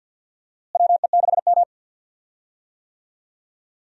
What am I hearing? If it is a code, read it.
Morse code, heard as a keyed tone, about 47 wpm, 700 Hz. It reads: WE6M
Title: USA Call Signs - 4 Characters - Repeated 3x - 50wpm